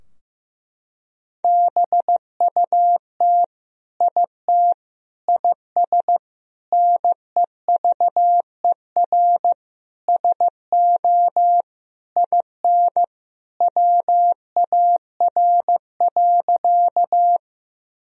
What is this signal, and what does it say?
Morse code, heard as a keyed tone, about 15 wpm, 700 Hz. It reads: BUT IT IS NEVER SO IN WAR.
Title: Sentences from Top 1000 Words - Review Part 2 - 15wpm